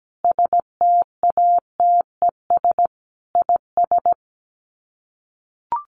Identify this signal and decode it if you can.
Morse code, heard as a keyed tone, about 17 wpm, 700 Hz. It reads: STATES IS
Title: Sets of 5 Words Encore2 17wpm